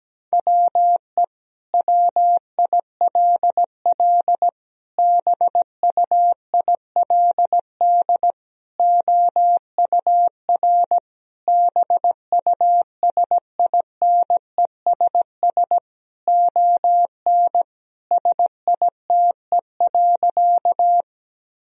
Morse code, heard as a keyed tone, about 17 wpm, 700 Hz. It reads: WE WILL BUILD OUR BUSINESS ON SITE.
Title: Sentences from Top 1000 Words - Review Part 4 - 17wpm